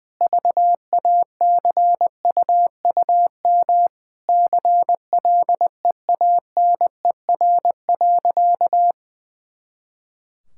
Morse code, heard as a keyed tone, about 19 wpm, 700 Hz. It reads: VACUUM CLEANER.
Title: Pangram Challenge v2 - 20wpm